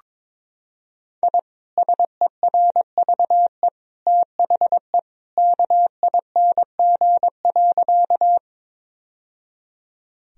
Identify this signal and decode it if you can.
Morse code, heard as a keyed tone, about 22 wpm, 700 Hz. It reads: I SERVE THE KING.
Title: Sentences from Top 1000 Words - Review Part 2 - 22wpm